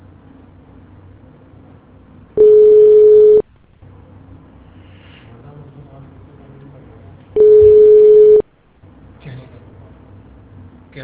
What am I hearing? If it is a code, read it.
Ringback tone (Europe, 425 Hz), with speech.